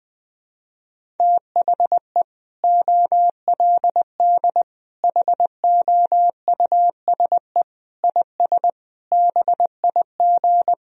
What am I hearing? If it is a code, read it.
Morse code, heard as a keyed tone, about 20 wpm, 700 Hz. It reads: THE OLD HOUSE IS BIG
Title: Sentences from Top 1000 Words - Review Part 1 - 20wpm